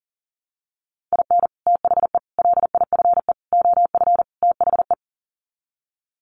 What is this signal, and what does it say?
Morse code, heard as a keyed tone, about 40 wpm, 700 Hz. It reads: IN THE LIFE OF THE
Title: Sets of 5 Words Encore2 40wpm